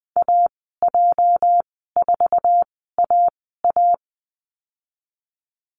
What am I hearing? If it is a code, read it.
Morse code, heard as a keyed tone, about 20 wpm, spaced as for 14 wpm, 700 Hz. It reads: AJ4AA
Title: 231 Lesson 37g - Letter x - Call Signs 20wpm